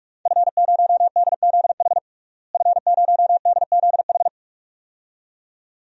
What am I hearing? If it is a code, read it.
Morse code, heard as a keyed tone, about 46 wpm, 700 Hz. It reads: F0DZH F0DZH
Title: DXCC-100 Call Signs - 5 Characters - Repeated 2x - 45wpm